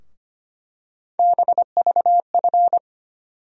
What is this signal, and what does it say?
Morse code, heard as a keyed tone, about 25 wpm, 700 Hz. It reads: BVF
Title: USA Call Sign Postfixes - 3 Letters 25wpm